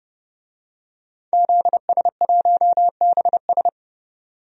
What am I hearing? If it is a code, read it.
Morse code, heard as a keyed tone, about 29 wpm, 700 Hz. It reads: ZS1BS
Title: DXCC-100 Call Signs - 5 Characters - 30wpm